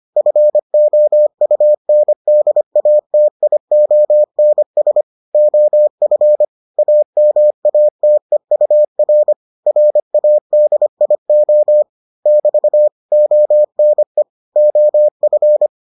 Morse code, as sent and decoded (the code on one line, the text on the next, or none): ..-. --- ..- -. -.. .- - .. --- -. ... / --- ..-. / .- -- .- - . ..- .-. / .-. .- -.. .. --- / -...- / --- -. / --- ..-.
FOUNDATIONS OF AMATEUR RADIO <BT> ONE OF